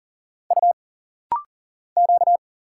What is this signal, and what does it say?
Morse code, heard as a keyed tone, about 39 wpm, 700 Hz. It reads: U Q
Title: Single Letters - Rapid-Fire 40wpm